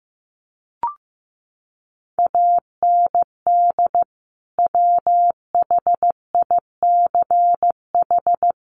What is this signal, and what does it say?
Morse code, heard as a keyed tone, about 15 wpm, 700 Hz. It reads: AND WHICH
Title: Sentences from Top 1400 Words 15wpm